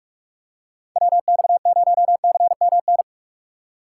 Morse code, heard as a keyed tone, about 44 wpm, 700 Hz. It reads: WX0CMN
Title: USA Call Signs - 6 Characters - 6x Character Spacing - 45wpm